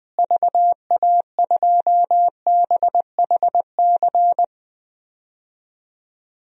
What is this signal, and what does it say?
Morse code, heard as a keyed tone, about 20 wpm, 700 Hz. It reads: VA2BHC
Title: DXCC-100 Call Signs - 6 Characters - Repeated 2x - 20wpm